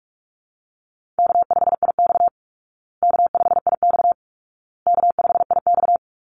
Morse code, heard as a keyed tone, about 44 wpm, 700 Hz. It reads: K5IX K5IX K5IX
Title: Call Signs Repeated 2x - 45wpm